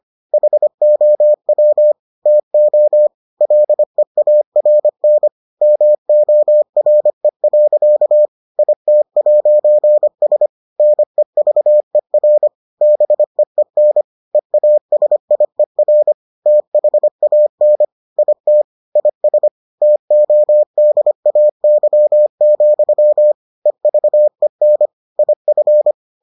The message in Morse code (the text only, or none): HOW TO LEARN MORE. IT'S NEVER BEEN EASIER THAN IT IS TODAY, EVEN IF